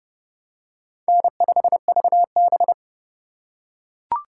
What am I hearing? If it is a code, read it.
Morse code, heard as a keyed tone, about 31 wpm, 700 Hz. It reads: N5VB
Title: USA Call Signs - 4 Characters - 4x Character Spacing - 30wpm